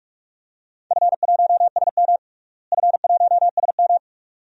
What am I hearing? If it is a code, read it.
Morse code, heard as a keyed tone, about 44 wpm, 700 Hz. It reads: F1SM F1SM
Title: DXCC-100 Call Signs - 4 Characters - Repeated 2x - 45wpm